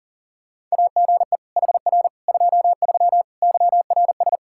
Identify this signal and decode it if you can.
Morse code, heard as a keyed tone, about 40 wpm, 700 Hz. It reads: AGE HR 23 YRS
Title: Practice QSO - Exchange 3 - More Info - 40wpm